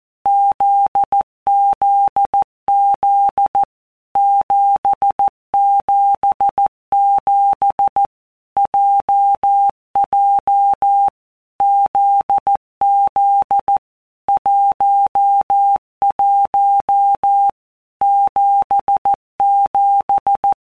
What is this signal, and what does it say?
Morse code, heard as a keyed tone, about 14 wpm, 800 Hz. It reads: ZZZ 777 JJ ZZ 11 77
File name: Basic Letter Groups By WPM's / Groups@13 / JZ17@13